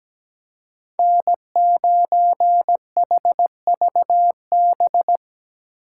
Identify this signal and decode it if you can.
Morse code, heard as a keyed tone, about 17 wpm, 700 Hz. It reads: N9HVB
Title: USA Call Signs - 5 Characters - 6x Character Spacing - 17wpm